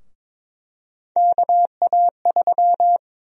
Morse code, heard as a keyed tone, about 22 wpm, 700 Hz. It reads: KA3
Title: USA Call Sign Prefixes - 22wpm